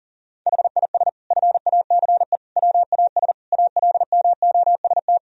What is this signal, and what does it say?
Morse code, heard as a keyed tone, about 40 wpm, 700 Hz. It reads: HIS FACE WAS ALMOST
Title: Sentences from Top 1000 Words - Review Part 4 - 40wpm